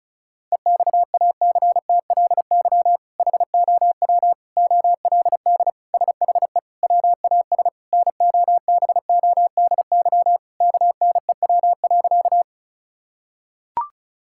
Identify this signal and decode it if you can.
Morse code, heard as a keyed tone, about 34 wpm, 700 Hz. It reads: EXACTLY HOW OLD SHE WAS NOBODY KNEW.
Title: Sentences from Top 1600 Words 35wpm